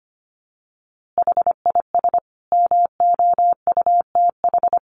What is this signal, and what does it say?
Morse code, heard as a keyed tone, about 25 wpm, 700 Hz. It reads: HIS MOUTH
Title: A Sound Of Thunder 25wpm